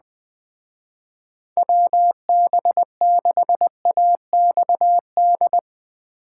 Morse code, heard as a keyed tone, about 20 wpm, 700 Hz. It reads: WB6AXD